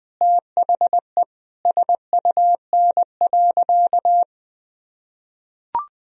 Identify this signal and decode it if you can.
Morse code, heard as a keyed tone, about 20 wpm, 700 Hz. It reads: THE SUN.
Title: Sentences from Top 1000 Words - Review Part 1 - 20wpm